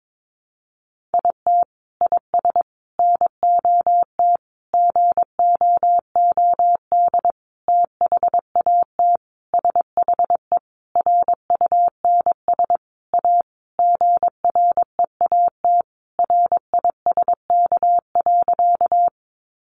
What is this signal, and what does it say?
Morse code, heard as a keyed tone, about 22 wpm, 700 Hz. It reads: IT IS NOT GOOD THAT SHE RUNS A GREAT RISK.
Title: Sentences from Top 600 Words 22wpm